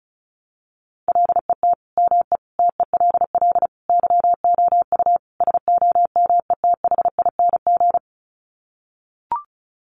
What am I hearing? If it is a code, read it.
Morse code, heard as a keyed tone, about 34 wpm, 700 Hz. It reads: LET ME TELL YOU SOMETHING
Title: Sets of 5 Words 35wpm